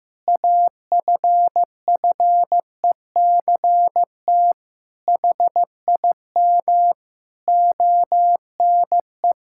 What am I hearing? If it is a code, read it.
Morse code, heard as a keyed tone, about 15 wpm, 700 Hz. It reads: AFFECT HIM ONE
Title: Sentences from Top 1000 Words - Review Part 6 - 15wpm